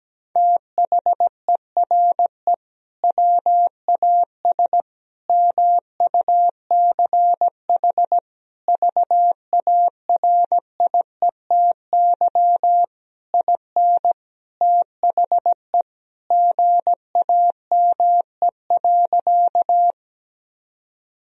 Morse code, heard as a keyed tone, about 17 wpm, 700 Hz. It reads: THERE WAS MUCH VARIETY IN THE GAME.
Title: Sentences from Top 1000 Words - Review Part 4 - 17wpm